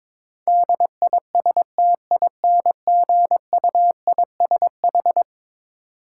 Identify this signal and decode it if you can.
Morse code, heard as a keyed tone, about 22 wpm, 700 Hz. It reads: DISTINGUISH
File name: practice / 22wpm / Eleven Letter Words - 22wpm